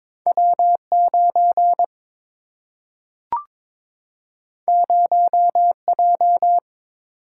Morse code, heard as a keyed tone, about 22 wpm, 700 Hz. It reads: W 9 0 J
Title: Single-Letter Single-Number 22wpm